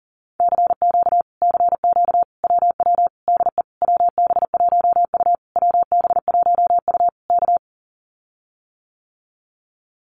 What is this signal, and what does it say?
Morse code, heard as a keyed tone, about 40 wpm, 700 Hz. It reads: CQ CQ WW DE WB1U WB1U K
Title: Common QSO Phrases Encore 40wpm